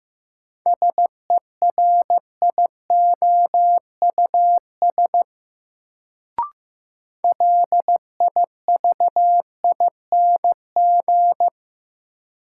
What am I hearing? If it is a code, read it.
Morse code, heard as a keyed tone, about 16 wpm, 700 Hz. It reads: SERIOUS LIVING
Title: Sentences from Top 1100 Words - New Word Drill Alt - 15wpm